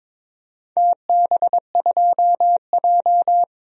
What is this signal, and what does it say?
Morse code, heard as a keyed tone, about 22 wpm, 700 Hz. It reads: TB2J
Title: DXCC-100 Call Signs - 4 Characters - 22wpm